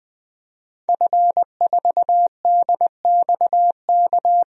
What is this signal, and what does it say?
Morse code, heard as a keyed tone, about 20 wpm, 700 Hz. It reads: F4DXK
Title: DXCC-100 Call Signs - 5 Characters - Repeated 2x - 20wpm